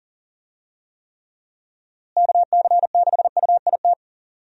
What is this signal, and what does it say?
Morse code, heard as a keyed tone, about 40 wpm, 700 Hz. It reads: KC6UIT
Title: USA Call Signs - 6 Characters - Repeated 2x - 40wpm